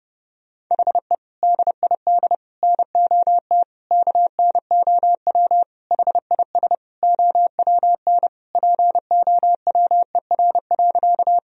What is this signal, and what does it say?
Morse code, heard as a keyed tone, about 32 wpm, 700 Hz. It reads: HE DID NOT KNOW HIS OWN POWER.
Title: Sentences from Top 1000 Words - Review Part 2 - 30wpm